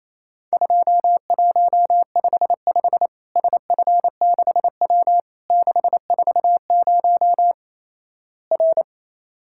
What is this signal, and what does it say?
Morse code, heard as a keyed tone, about 28 wpm, 700 Hz. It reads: 2155 SF6W 640 R
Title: Worked All Europe DX Contest - QTC Exchange - 28wpm